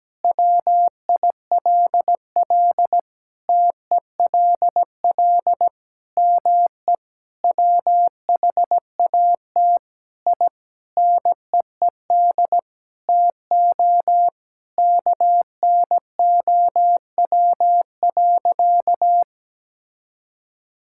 Morse code, heard as a keyed tone, about 17 wpm, 700 Hz. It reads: WILL TELL ME WHAT I NEED TO KNOW.
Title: Sentences from Top 1000 Words - Review Part 5 - 17wpm